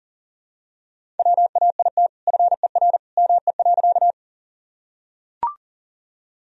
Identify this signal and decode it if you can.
Morse code, heard as a keyed tone, about 37 wpm, 700 Hz. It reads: WAIT FER ME.
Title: Sentences from Top 500 Words and Abbreviations 40wpm